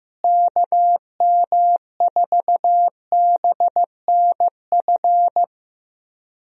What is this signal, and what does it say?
Morse code, heard as a keyed tone, about 16 wpm, 700 Hz. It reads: KM4BNF
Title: USA Call Signs - 6 Characters - Repeated 3x - 15wpm